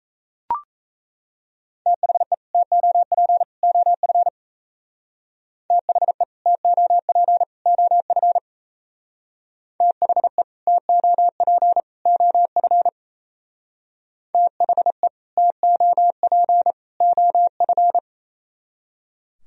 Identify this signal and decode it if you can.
Morse code, heard as a keyed tone, about 37 wpm, 700 Hz. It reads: THE TOP OF THE TOP OF THE TOP OF THE TOP OF